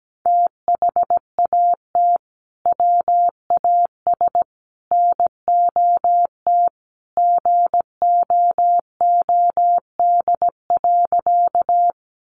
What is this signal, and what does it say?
Morse code, heard as a keyed tone, about 17 wpm, 700 Hz. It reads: THAT WAS NOT GOOD.
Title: Sentences from Top 600 Words 17wpm